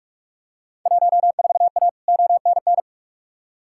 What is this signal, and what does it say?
Morse code, heard as a keyed tone, about 42 wpm, 700 Hz. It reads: 14A ONN
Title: Field Day Exchanges Encore - 45wpm